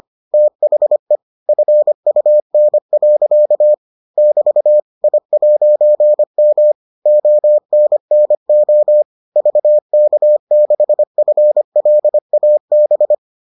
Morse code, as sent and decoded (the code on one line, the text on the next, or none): - .... / ..-. ..- -. .-.-.- / -...- / .. .----. -- / --- -. -. --- / ...- -.- -.... ..-. .-.. .- -...
THE FUN. <BT> I'M ONNO VK6FLAB